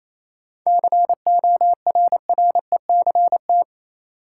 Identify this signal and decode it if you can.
Morse code, heard as a keyed tone, about 28 wpm, 700 Hz. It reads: CORRECT
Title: Sentences from Top 700 Words - New Word Drill - 28wpm